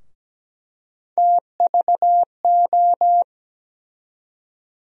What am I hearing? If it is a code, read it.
Morse code, heard as a keyed tone, about 17 wpm, 700 Hz. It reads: TVO
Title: USA Call Sign Postfixes - 3 Letters 17wpm